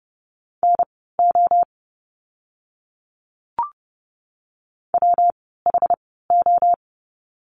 Morse code, heard as a keyed tone, about 30 wpm, spaced as for 17 wpm, 700 Hz. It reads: NO WHO
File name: course / 30:10wpm / 115 Lesson 19f - Letter w - Words 30wpm